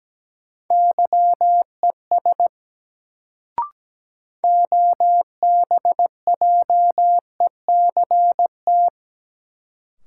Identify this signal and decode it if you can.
Morse code, heard as a keyed tone, about 17 wpm, 700 Hz. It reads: YES OBJECT
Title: Sentences from Top 500 Words - New Word Drill - 17wpm